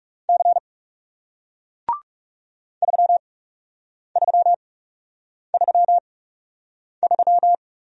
Morse code, heard as a keyed tone, about 38 wpm, spaced as for 8 wpm, 700 Hz. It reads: C 3333